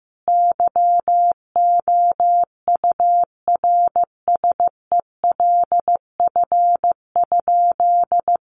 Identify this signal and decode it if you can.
Morse code, heard as a keyed tone, about 15 wpm, 700 Hz. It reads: YOURSELF?
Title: Sentences from Top 1500 Words 15wpm